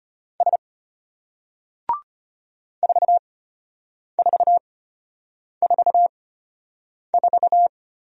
Morse code, heard as a keyed tone, about 34 wpm, spaced as for 8 wpm, 700 Hz. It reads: S 4444